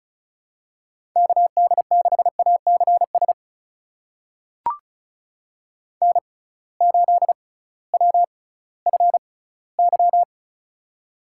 Morse code, heard as a keyed tone, about 36 wpm, spaced as for 19 wpm, 700 Hz. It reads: KD6ACS N8WFY
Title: USA Call Signs - 6x Character Spacing - 35wpm